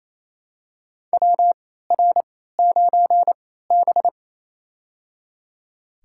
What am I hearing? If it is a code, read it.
Morse code, heard as a keyed tone, about 28 wpm, spaced as for 16 wpm, 700 Hz. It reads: WR9B
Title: USA Call Signs - 4 Characters - 3x Character Spacing - 28wpm